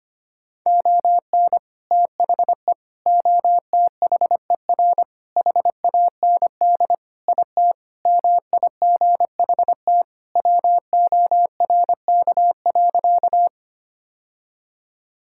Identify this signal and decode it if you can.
Morse code, heard as a keyed tone, about 25 wpm, 700 Hz. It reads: ON THE OTHER HAND IT MIGHT WORK.
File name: practice / 25wpm / Sentences with 3-Word Phrases - 25wpm